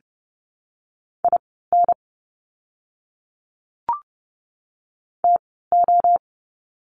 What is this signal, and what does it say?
Morse code, heard as a keyed tone, about 30 wpm, spaced as for 17 wpm, 700 Hz. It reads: IN TO
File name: course / 30:10wpm / 115 Lesson 19f - Letter w - Words 30wpm